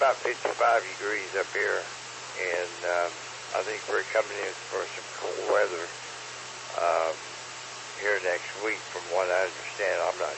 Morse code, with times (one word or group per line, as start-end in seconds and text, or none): none